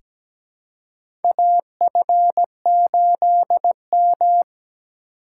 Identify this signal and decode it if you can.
Morse code, heard as a keyed tone, about 17 wpm, 700 Hz. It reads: AF8M
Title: USA Call Signs - 4 Characters - 3x Character Spacing - 17wpm